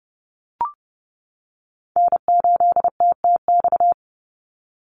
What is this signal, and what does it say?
Morse code, heard as a keyed tone, about 30 wpm, 700 Hz. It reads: N8TTX
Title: USA Call Signs 30wpm